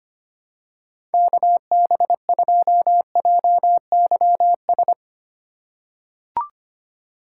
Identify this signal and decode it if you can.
Morse code, heard as a keyed tone, about 25 wpm, 700 Hz. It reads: KB2JYS
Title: USA Call Signs - 4x Character Spacing - 25wpm